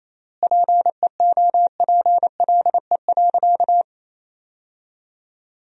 Morse code, heard as a keyed tone, about 28 wpm, 700 Hz. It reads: PEOPLE.
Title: 4x Word-Spacing - Sentences from Top 200 Words - 28wpm